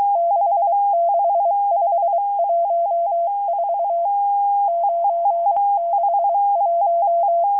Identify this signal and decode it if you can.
Morse code, heard as a keyed tone, about 23 wpm, 680 Hz. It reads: 66514 961